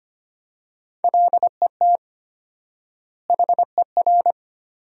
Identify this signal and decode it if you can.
Morse code, heard as a keyed tone, about 25 wpm, 700 Hz. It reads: LET HER
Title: 4x Word-Spacing - Sentences from Top 300 Words - 25wpm